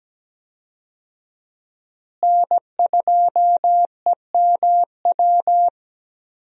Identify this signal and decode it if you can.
Morse code, heard as a keyed tone, about 17 wpm, 700 Hz. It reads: N2EMW
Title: USA Call Signs - 5 Characters - Repeated 2x - 17wpm